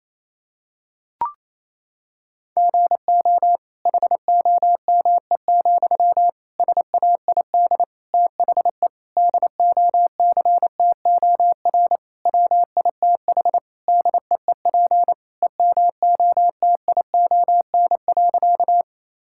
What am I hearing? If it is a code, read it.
Morse code, heard as a keyed tone, about 28 wpm, 700 Hz. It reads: GO HOME, SAID THE DOCTOR WITH DEEP EMOTION.
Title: Sentences from Top 1600 Words 28wpm